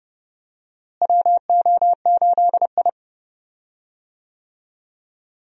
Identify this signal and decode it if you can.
Morse code, heard as a keyed tone, about 32 wpm, 700 Hz. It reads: WO8I
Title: USA Call Signs - 4 Characters - Repeated 3x - 30wpm